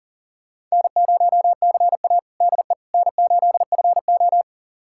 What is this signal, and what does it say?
Morse code, heard as a keyed tone, about 40 wpm, 690 Hz. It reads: N0CA DE N8FO
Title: USA Callsign DE Callsign - 4 Characters - 40wpm